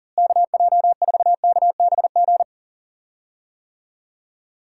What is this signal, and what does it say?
Morse code, heard as a keyed tone, about 40 wpm, 700 Hz. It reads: KJ4KBG
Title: USA Call Signs - 6 Characters - Repeated 2x - 40wpm